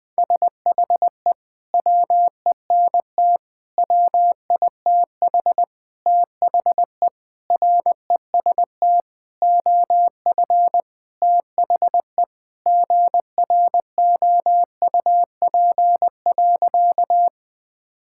Morse code, heard as a keyed tone, about 20 wpm, 700 Hz. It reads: SHE WENT WITH THE REST OF THE GROUP.
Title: Sentences with 5-Word Phrases - 20wpm